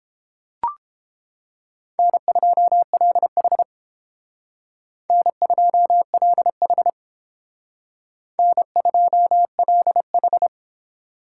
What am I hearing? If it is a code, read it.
Morse code, heard as a keyed tone, about 30 wpm, 700 Hz. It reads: N2LH N2LH N2LH